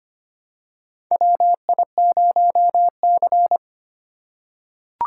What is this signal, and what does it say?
Morse code, heard as a keyed tone, about 25 wpm, 700 Hz. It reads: WI0C
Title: USA Call Signs - 4 Characters - 2x Character Spacing - 25wpm